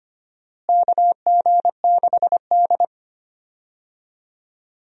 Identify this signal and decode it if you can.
Morse code, heard as a keyed tone, about 25 wpm, 700 Hz. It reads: KG6D
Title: USA Call Signs - 4 Characters - Repeated 2x - 25wpm